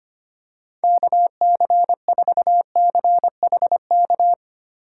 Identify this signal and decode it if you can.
Morse code, heard as a keyed tone, about 25 wpm, 700 Hz. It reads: KC4CHK
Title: USA Call Signs - 8x Character Spacing - 25wpm